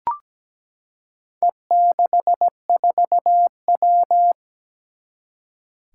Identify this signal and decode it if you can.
Morse code, heard as a keyed tone, about 17 wpm, 700 Hz. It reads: E64W
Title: Four Challenging Letter-Number Combinations 17wpm